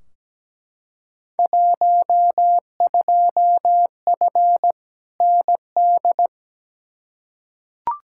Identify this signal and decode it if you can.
Morse code, heard as a keyed tone, about 17 wpm, 700 Hz. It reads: 12F ND
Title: Field Day Exchanges Encore - 17wpm